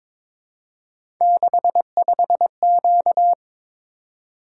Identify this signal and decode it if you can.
Morse code, heard as a keyed tone, about 23 wpm, 700 Hz. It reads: 65Q
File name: practice / 22wpm / Three Letter-Number Combinations - 22wpm